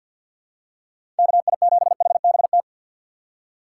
Morse code, heard as a keyed tone, about 47 wpm, 700 Hz. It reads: KI7HBT
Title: USA Call Signs - 6 Characters - 3x Character Spacing - 50wpm